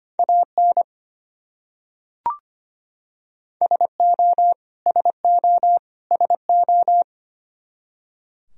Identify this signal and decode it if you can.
Morse code, heard as a keyed tone, about 25 wpm, 700 Hz. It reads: AN SO SO SO
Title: Top 100 Words Repeated 25wpm